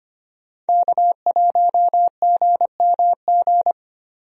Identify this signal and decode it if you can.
Morse code, heard as a keyed tone, about 25 wpm, 700 Hz. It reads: K1GMG
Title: USA Call Signs - 5 Characters - Repeated 2x - 25wpm